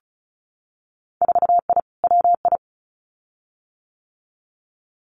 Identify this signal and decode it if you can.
Morse code, heard as a keyed tone, about 36 wpm, 700 Hz. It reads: 4I WI
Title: Winter Field Day Exchanges 35wpm